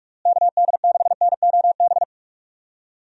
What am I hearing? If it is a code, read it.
Morse code, heard as a keyed tone, about 47 wpm, 700 Hz. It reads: KD6NOB
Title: USA Call Signs - 6 Characters - Repeated 3x - 45wpm